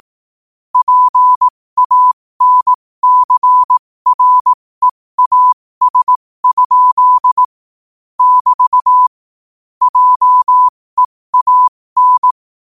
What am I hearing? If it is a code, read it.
Morse code, heard as a keyed tone, about 18 wpm, spaced as for 15 wpm, 1000 Hz. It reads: PANCREAS? <BT> JEAN
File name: qotdfiles / 15wpm